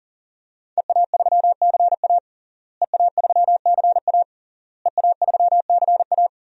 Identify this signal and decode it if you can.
Morse code, heard as a keyed tone, about 40 wpm, 700 Hz. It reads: EA3CA EA3CA EA3CA
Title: DXCC-100 Call Signs - 5 Characters - Repeated 3x - 40wpm